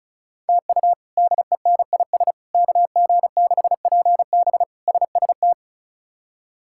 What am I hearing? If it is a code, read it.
Morse code, heard as a keyed tone, about 36 wpm, 700 Hz. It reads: TU DENIS KG6PB SST
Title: K1USN SST QSO Practice - 35wpm